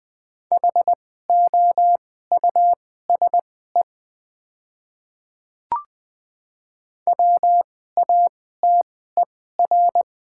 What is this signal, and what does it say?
Morse code, heard as a keyed tone, about 20 wpm, spaced as for 14 wpm, 700 Hz. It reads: HOUSE WATER
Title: 2x Character Spacing - Top 200 Words - 20wpm